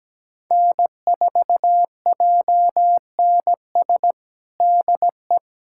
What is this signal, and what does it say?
Morse code, heard as a keyed tone, about 17 wpm, 700 Hz. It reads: N4JNS DE
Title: USA Callsign DE Callsign 17wpm